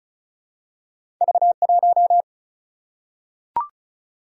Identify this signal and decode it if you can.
Morse code, heard as a keyed tone, about 35 wpm, 700 Hz. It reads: V1
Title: Single-Letter Single-Number 35wpm